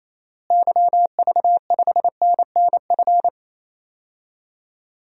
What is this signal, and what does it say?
Morse code, heard as a keyed tone, about 27 wpm, 700 Hz. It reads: YV5NNF
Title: DXCC-100 Call Signs - 6 Characters - Repeated 3x - 28wpm